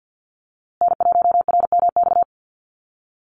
Morse code, heard as a keyed tone, about 50 wpm, 700 Hz. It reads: N1RMX
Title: USA Call Signs Repeated 2x 50wpm